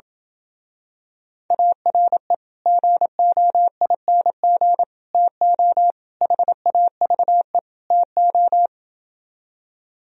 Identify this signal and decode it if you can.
Morse code, heard as a keyed tone, about 27 wpm, 700 Hz. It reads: ARE GOING TO HAVE TO